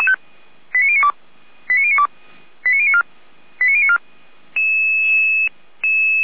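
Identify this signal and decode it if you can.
Five-tone selective call (ZVEI), heard as quick runs of five tones, tones of about 70 ms. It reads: 89092 89092 89094 89094
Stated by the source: ZVEI five-tone selective call received by SDR.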